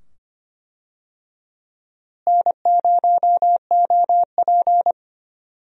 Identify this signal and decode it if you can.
Morse code, heard as a keyed tone, about 25 wpm, 700 Hz. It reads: N0OP
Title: USA Call Signs - 4 Characters - Repeated 2x - 25wpm